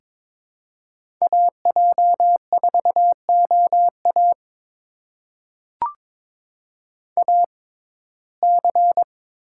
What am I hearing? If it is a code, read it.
Morse code, heard as a keyed tone, about 22 wpm, spaced as for 16 wpm, 700 Hz. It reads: AJ4OA AC
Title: USA Call Signs - 5 Characters - 6x Character Spacing - 22wpm